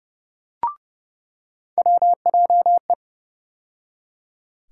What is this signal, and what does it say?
Morse code, heard as a keyed tone, about 30 wpm, 700 Hz. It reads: WJE
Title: USA Call Sign Postfixes - 3 Letters 30wpm